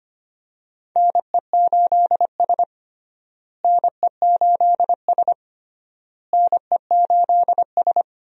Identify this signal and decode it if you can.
Morse code, heard as a keyed tone, about 25 wpm, 700 Hz. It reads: NE8S NE8S NE8S
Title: USA Call Signs - 4 Characters - Repeated 3x - 25wpm